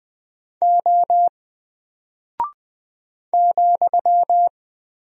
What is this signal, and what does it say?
Morse code, heard as a keyed tone, about 20 wpm, 700 Hz. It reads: O ,
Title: Single Character (Letter:Number:Punctuation:Prosign) 20wpm